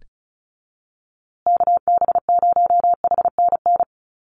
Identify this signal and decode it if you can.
Morse code, heard as a keyed tone, about 34 wpm, 700 Hz. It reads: KB0HNN
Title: USA Call Signs Repeated 2x 35wpm